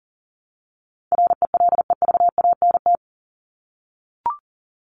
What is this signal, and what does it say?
Morse code, heard as a keyed tone, about 40 wpm, 700 Hz. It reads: RELEVANT
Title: Top 1001-1250 Words 40wpm